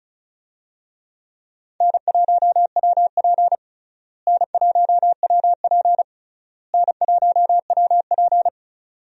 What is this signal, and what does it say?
Morse code, heard as a keyed tone, about 34 wpm, 700 Hz. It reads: N1WP N1WP N1WP
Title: USA Call Signs - 4 Characters - Repeated 3x - 35wpm